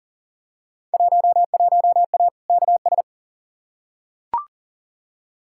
Morse code, heard as a keyed tone, about 40 wpm, 700 Hz. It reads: 11A KS
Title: Field Day Exchanges Encore - 40wpm